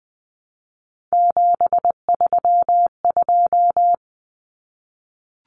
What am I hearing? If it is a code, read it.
Morse code, heard as a keyed tone, about 20 wpm, 700 Hz. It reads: 732
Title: Numbers - 3-Digit 20wpm